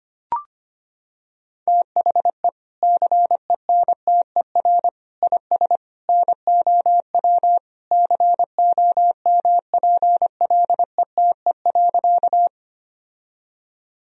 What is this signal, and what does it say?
Morse code, heard as a keyed tone, about 25 wpm, 700 Hz. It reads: THE CENTER IS NOW COMPLETE.
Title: Sentences from Top 1000 Words - Review Part 2 - 25wpm